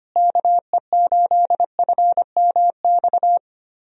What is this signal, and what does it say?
Morse code, heard as a keyed tone, about 25 wpm, 700 Hz. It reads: KE8FMX
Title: USA Call Signs - 6 Characters - Repeated 3x - 25wpm